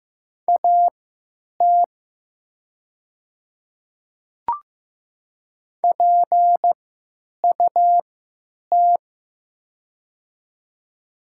Morse code, heard as a keyed tone, about 15 wpm, spaced as for 9 wpm, 700 Hz. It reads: AT PUT
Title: 3x Character Spacing - Top 300 Words - 15wpm